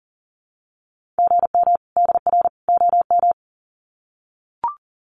Morse code, heard as a keyed tone, about 40 wpm, 700 Hz. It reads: GM DR OM
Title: Common QSO Phrases 40wpm